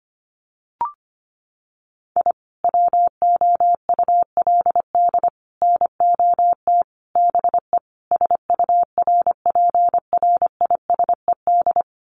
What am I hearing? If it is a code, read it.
Morse code, heard as a keyed tone, about 25 wpm, 700 Hz. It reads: I WOULD NOT BE SURPRISED
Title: Sets of 5 Words Encore2 25wpm